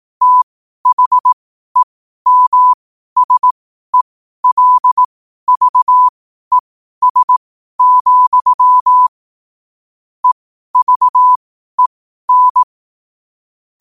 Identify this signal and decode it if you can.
Morse code, heard as a keyed tone, about 18 wpm, spaced as for 12 wpm, 1000 Hz. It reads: THEMSELVES, EVEN
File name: qotdfiles / 12wpm